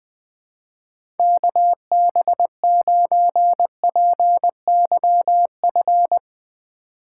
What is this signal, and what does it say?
Morse code, heard as a keyed tone, about 20 wpm, 700 Hz. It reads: KB9PYF
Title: USA Call Signs - 6 Characters - 6x Character Spacing - 20wpm